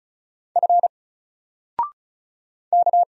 Morse code, heard as a keyed tone, about 36 wpm, 700 Hz. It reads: F K
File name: Single-Letter-Number-35wpm